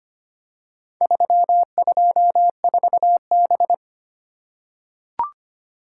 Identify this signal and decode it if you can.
Morse code, heard as a keyed tone, about 25 wpm, 700 Hz. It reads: 324B
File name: practice / 25wpm / Four Letter-Number Combinations - 25wpm